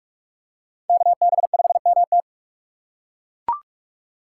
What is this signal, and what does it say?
Morse code, heard as a keyed tone, about 42 wpm, 700 Hz. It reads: KB5MT